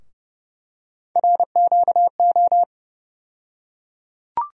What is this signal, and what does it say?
Morse code, heard as a keyed tone, about 30 wpm, 700 Hz. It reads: RQO
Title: USA Call Sign Postfixes - 3 Letters 30wpm